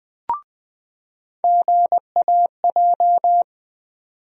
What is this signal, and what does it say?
Morse code, heard as a keyed tone, about 20 wpm, 700 Hz. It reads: GAJ
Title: USA Call Sign Postfixes - 3 Letters 20wpm